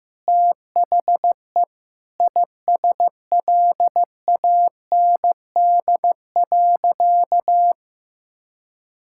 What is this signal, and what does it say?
Morse code, heard as a keyed tone, about 15 wpm, 700 Hz. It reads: THE ISLAND.
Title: Sentences from Top 1000 Words - Review Part 6 - 15wpm